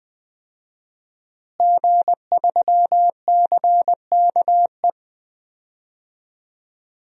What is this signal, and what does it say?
Morse code, heard as a keyed tone, about 21 wpm, 700 Hz. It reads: G3CKE
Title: DXCC-100 Call Signs - 5 Characters - Repeated 2x - 20wpm